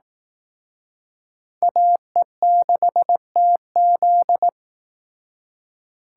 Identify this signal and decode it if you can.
Morse code, heard as a keyed tone, about 19 wpm, 700 Hz. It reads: AE6TZ